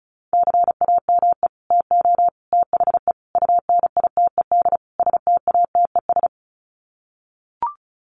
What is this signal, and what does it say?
Morse code, heard as a keyed tone, about 34 wpm, 700 Hz. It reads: CAME TO THE UNITED STATES
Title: Sets of 5 Words Encore2 35wpm